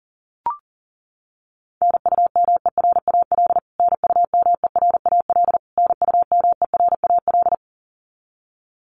Morse code, heard as a keyed tone, about 40 wpm, 700 Hz. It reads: NUMERAL NUMERAL NUMERAL
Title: Top 500 Words Repeated 40wpm